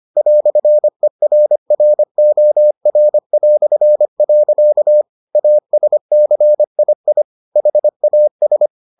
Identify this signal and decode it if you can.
Morse code, heard as a keyed tone, about 25 wpm, 600 Hz. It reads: "ERROR". ASCII HAS